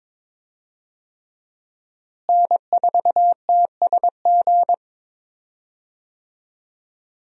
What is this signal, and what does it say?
Morse code, heard as a keyed tone, about 21 wpm, 700 Hz. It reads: N4TSG
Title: USA Call Signs - 5 Characters - Repeated 3x - 22wpm